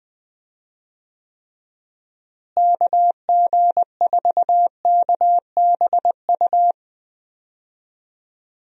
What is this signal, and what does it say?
Morse code, heard as a keyed tone, about 20 wpm, 700 Hz. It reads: KG4KBU
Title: USA Call Signs - 6 Characters - Repeated 2x - 20wpm